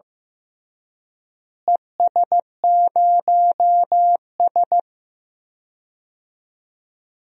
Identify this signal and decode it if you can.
Morse code, heard as a keyed tone, about 15 wpm, 700 Hz. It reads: ES0S
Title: DXCC-100 Call Signs - 4 Characters - Repeated 3x - 15wpm